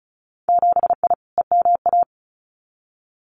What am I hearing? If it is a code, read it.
Morse code, heard as a keyed tone, about 34 wpm, 700 Hz. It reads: 7I EMA
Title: Winter Field Day Exchanges Encore 35wpm